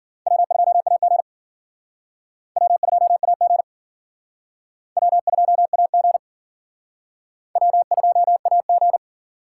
Morse code, heard as a keyed tone, about 54 wpm, 700 Hz. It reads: W2AG W2AG W2AG W2AG